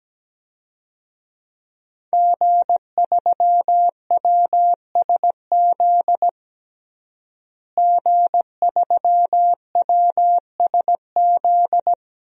Morse code, heard as a keyed tone, about 17 wpm, 700 Hz. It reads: G3WSZ G3WSZ
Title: DXCC-100 Call Signs - 5 Characters - Repeated 3x - 17wpm